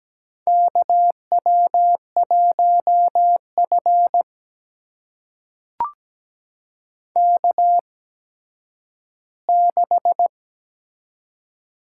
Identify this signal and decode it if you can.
Morse code, heard as a keyed tone, about 17 wpm, spaced as for 10 wpm, 700 Hz. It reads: KW1F K6
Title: USA Call Signs - 4 Characters - 8x Character Spacing - 17wpm